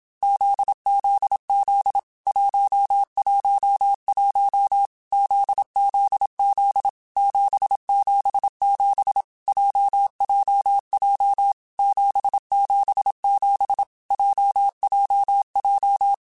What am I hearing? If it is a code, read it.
Morse code, heard as a keyed tone, about 27 wpm, 800 Hz. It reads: ZZZ 111 ZZZ 777 JJJ 777 JJJ
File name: Groups@25 / Groups@25 mp3s / JZ17@25